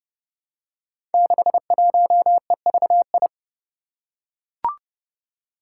Morse code, heard as a keyed tone, about 28 wpm, 700 Hz. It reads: 61EVI
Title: Five Challenging Letter-Number Combinations 30wpm